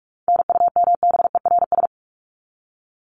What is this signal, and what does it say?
Morse code, heard as a keyed tone, about 44 wpm, 700 Hz. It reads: NUMBERS
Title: ES:S Suffix - Top 100 Words with Base Words 45wpm